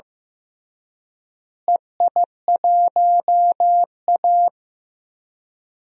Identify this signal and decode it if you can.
Morse code, heard as a keyed tone, about 16 wpm, 700 Hz. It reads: EI1A
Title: DXCC-100 Call Signs - 4 Characters - Repeated 3x - 15wpm